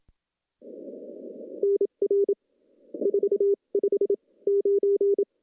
Morse code, heard as a keyed tone, about 27 wpm, spaced as for 21 wpm, 410 Hz. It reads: NR 459
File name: CW-SS-k5tr